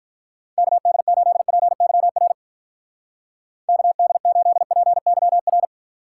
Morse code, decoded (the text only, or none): KD8PYR KD8PYR